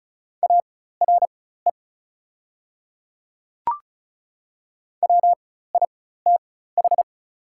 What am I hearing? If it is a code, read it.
Morse code, heard as a keyed tone, about 36 wpm, spaced as for 16 wpm, 700 Hz. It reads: ARE WITH
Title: 4x Character Spacing - Top 100 Words - 35wpm